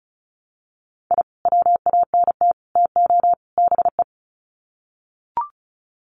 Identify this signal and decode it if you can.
Morse code, heard as a keyed tone, about 34 wpm, 700 Hz. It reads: I WANT TO BE
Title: Sets of 4 Words 35wpm